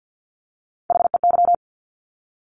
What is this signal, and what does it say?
Morse code, heard as a keyed tone, about 50 wpm, 700 Hz. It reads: HEY
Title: Top Three Letter Combinations in Words 50wpm